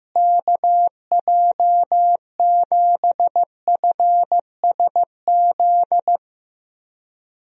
Morse code, heard as a keyed tone, about 15 wpm, 700 Hz. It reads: KJ7FSZ
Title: USA Call Signs - 6 Characters - Repeated 2x - 15wpm